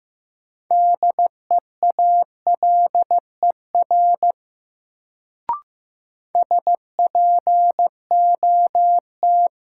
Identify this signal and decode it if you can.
Morse code, heard as a keyed tone, about 15 wpm, 700 Hz. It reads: DEALER SPOT
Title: Sentences from Top 1100 Words - New Word Drill Alt - 15wpm